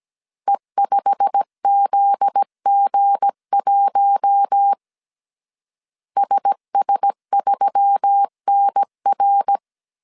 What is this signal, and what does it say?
Morse code, heard as a keyed tone, about 17 wpm, 800 Hz. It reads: E5ZG1 SS3NR
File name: CW / DET / 1 9 1500C S C 16wpm 2909